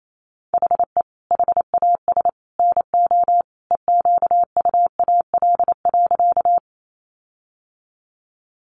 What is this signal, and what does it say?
Morse code, heard as a keyed tone, about 28 wpm, 700 Hz. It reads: HE HAS NO EQUAL.
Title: Sentences from Top 1400 Words 28wpm